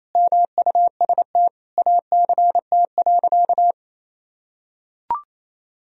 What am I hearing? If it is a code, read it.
Morse code, heard as a keyed tone, about 28 wpm, 700 Hz. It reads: MUST ACT.
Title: Sentences from Top 1000 Words - Review Part 1 - 28wpm